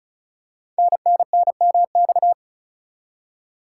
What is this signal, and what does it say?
Morse code, heard as a keyed tone, about 35 wpm, 690 Hz. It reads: NNNMX
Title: Code Groups - 5 Letters - 35wpm